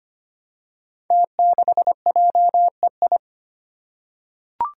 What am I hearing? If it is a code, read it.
Morse code, heard as a keyed tone, about 25 wpm, 700 Hz. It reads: T6JEI
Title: Five Challenging Letter-Number Combinations 25wpm